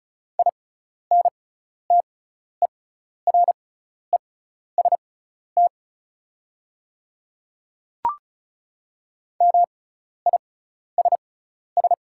Morse code, heard as a keyed tone, about 36 wpm, spaced as for 12 wpm, 690 Hz. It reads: INTEREST MISS
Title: 6x Character Spacing - Top 500 Words - 35wpm